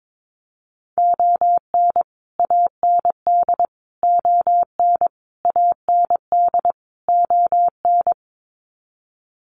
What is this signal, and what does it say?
Morse code, heard as a keyed tone, about 22 wpm, 700 Hz. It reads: ON AND ON AND ON
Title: Sets of 5 Words Encore2 22wpm